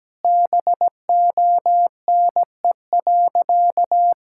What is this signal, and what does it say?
Morse code, heard as a keyed tone, about 17 wpm, 700 Hz. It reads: BONE.
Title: Sentences from Top 1600 Words 17wpm